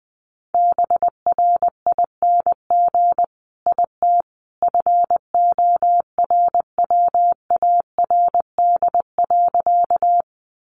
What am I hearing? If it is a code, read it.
Morse code, heard as a keyed tone, about 20 wpm, 700 Hz. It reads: BRING IT FORWARD.
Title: Sentences from Top 1300 Words 20wpm